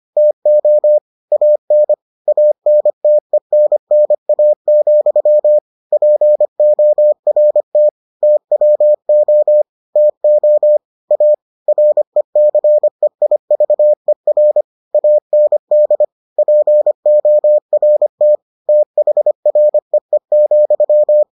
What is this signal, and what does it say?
Morse code, heard as a keyed tone, about 25 wpm, 600 Hz. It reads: TO AN ANTENNA, PORT TWO TO A RECEIVER AND PORT THREE,